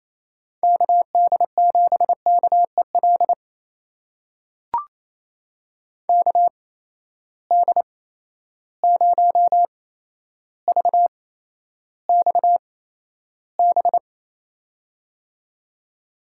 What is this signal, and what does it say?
Morse code, heard as a keyed tone, about 27 wpm, spaced as for 12 wpm, 700 Hz. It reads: KD7KEL KD0VXB
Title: USA Call Signs - 6 Characters - 8x Character Spacing - 28wpm